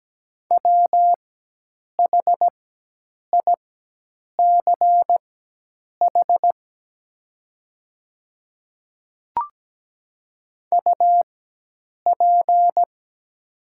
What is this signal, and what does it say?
Morse code, heard as a keyed tone, about 17 wpm, spaced as for 8 wpm, 700 Hz. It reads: WHICH UP
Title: 4x Character Spacing - Top 200 Words - 17wpm